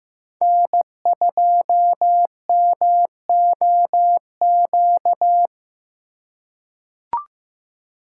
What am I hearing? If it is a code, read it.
Morse code, heard as a keyed tone, about 15 wpm, 700 Hz. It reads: N2MOQ
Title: USA Call Signs - 5 Characters - 2x Character Spacing - 15wpm